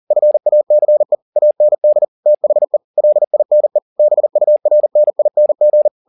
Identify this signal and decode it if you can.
Morse code, heard as a keyed tone, about 40 wpm, 600 Hz. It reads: FACE AND THE LINE BURNING
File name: the old man and the sea 2 40wpm